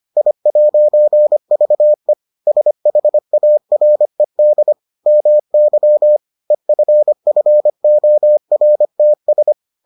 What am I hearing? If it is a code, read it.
Morse code, heard as a keyed tone, about 25 wpm, 600 Hz. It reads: I'VE SHARED MY EFFORTS